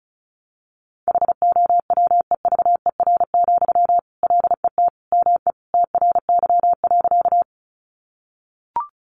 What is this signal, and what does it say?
Morse code, heard as a keyed tone, about 34 wpm, 700 Hz. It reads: HOWEVER, LET ME TRY.
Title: Sentences from Top 600 Words 35wpm